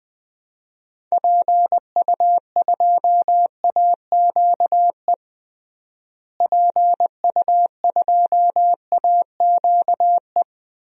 Morse code, heard as a keyed tone, about 21 wpm, 700 Hz. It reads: PU2AQE PU2AQE
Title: DXCC-100 Call Signs - 6 Characters - Repeated 2x - 20wpm